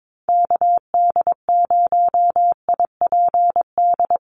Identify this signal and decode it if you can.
Morse code, heard as a keyed tone, about 22 wpm, 700 Hz. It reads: KD0IPD
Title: USA Call Signs 22wpm